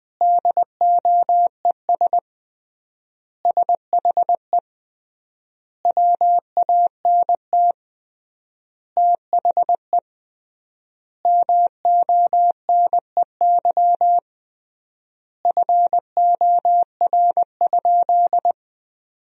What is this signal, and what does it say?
Morse code, heard as a keyed tone, about 20 wpm, 700 Hz. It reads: DOES SHE WANT THE MONEY FOR?
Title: 3x Word-Spacing - Sentences from Top 400 Words - 20wpm